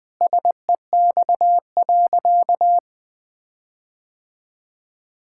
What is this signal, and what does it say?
Morse code, heard as a keyed tone, about 20 wpm, 700 Hz. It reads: SEX.
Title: Sentences from Top 1600 Words 20wpm